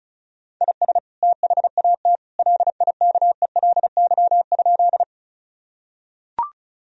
Morse code, heard as a keyed tone, about 34 wpm, 700 Hz. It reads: IS THAT LIKELY?
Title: Sentences from Top 1000 Words - Review Part 4 - 35wpm